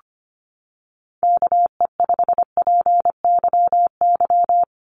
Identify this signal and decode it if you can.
Morse code, heard as a keyed tone, about 25 wpm, 700 Hz. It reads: KE5PYY
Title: USA Call Signs Repeated 2x 25wpm